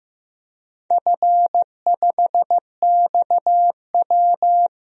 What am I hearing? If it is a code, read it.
Morse code, heard as a keyed tone, about 15 wpm, 700 Hz. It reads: F5XW
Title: DXCC-100 Call Signs - 4 Characters - 15wpm